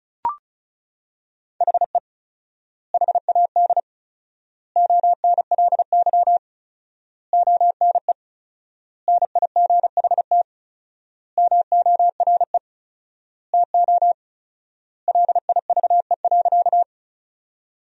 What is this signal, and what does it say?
Morse code, heard as a keyed tone, about 36 wpm, 700 Hz. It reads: HE HAD ONLY ONE NIGHT MORE TO LIVE.
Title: 4x Word-Spacing - Sentences from Top 300 Words - 35wpm